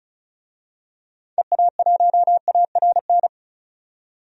Morse code, heard as a keyed tone, about 36 wpm, 700 Hz. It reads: EA1ARN
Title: DXCC-100 Call Signs - 6 Characters - 35wpm